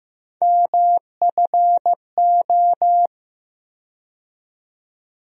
Morse code, heard as a keyed tone, about 15 wpm, 700 Hz. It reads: MFO
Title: USA Call Sign Postfixes - 3 Letters 15wpm